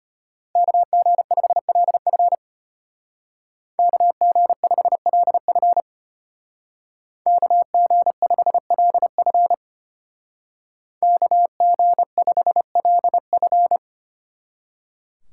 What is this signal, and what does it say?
Morse code, heard as a keyed tone, about 30 wpm, 700 Hz. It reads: KG5LF KG5LF KG5LF KG5LF
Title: Speed Racing - USA 2x2 Call Signs 25wpm